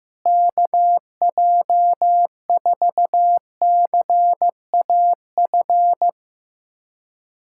Morse code, heard as a keyed tone, about 16 wpm, 700 Hz. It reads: KJ4CAF
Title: USA Callsign DE Callsign 15wpm